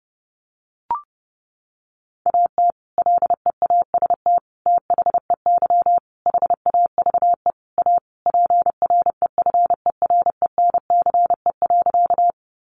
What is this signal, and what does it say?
Morse code, heard as a keyed tone, about 30 wpm, 700 Hz. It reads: AT LEAST THEY HAVE A PREFERENCE.
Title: Sentences from Top 1500 Words 30wpm